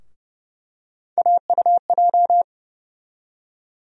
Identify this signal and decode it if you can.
Morse code, heard as a keyed tone, about 30 wpm, 700 Hz. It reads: AUJ
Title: USA Call Sign Postfixes - 3 Letters 30wpm